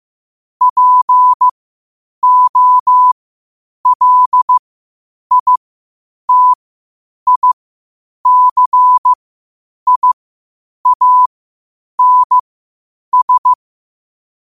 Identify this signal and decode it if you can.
Morse code, heard as a keyed tone, about 15 wpm, spaced as for 8 wpm, 1000 Hz. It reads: POLITICIANS
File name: qotdfiles / 8wpm